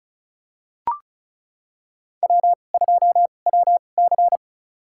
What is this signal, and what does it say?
Morse code, heard as a keyed tone, about 34 wpm, spaced as for 25 wpm, 700 Hz. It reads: W2WC
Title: USA Call Signs - 4 Characters - 2x Character Spacing - 35wpm